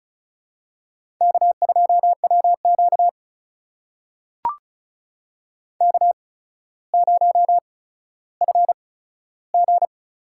Morse code, heard as a keyed tone, about 36 wpm, spaced as for 15 wpm, 700 Hz. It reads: K2WQ K0FG
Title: USA Call Signs - 8x Character Spacing - 35wpm